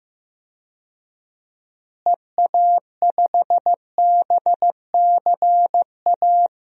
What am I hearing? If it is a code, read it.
Morse code, heard as a keyed tone, about 15 wpm, 700 Hz. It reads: EA5BCA
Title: DXCC-100 Call Signs - 6 Characters - Repeated 2x - 15wpm